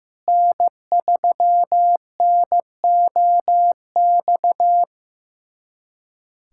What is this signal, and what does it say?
Morse code, heard as a keyed tone, about 15 wpm, 700 Hz. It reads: N3NOX
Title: USA Callsign DE Callsign - 5 Characters - 15wpm